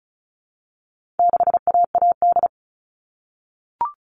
Morse code, heard as a keyed tone, about 34 wpm, 700 Hz. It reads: 6AAD
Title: All Cut Numbers - 4-Digit 35wpm